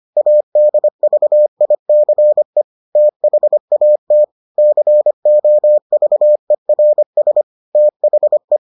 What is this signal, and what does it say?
Morse code, heard as a keyed tone, about 25 wpm, 600 Hz. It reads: ADVICE THAT COVERS THE